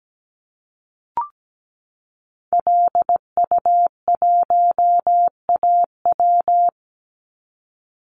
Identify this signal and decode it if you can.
Morse code, heard as a keyed tone, about 17 wpm, 700 Hz. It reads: LU1AW
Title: Call Signs Repeated 2x - 17wpm